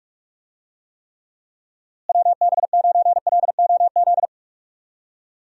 Morse code, heard as a keyed tone, about 46 wpm, 700 Hz. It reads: WB9LOZ
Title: USA Call Signs - 6 Characters - Repeated 2x - 45wpm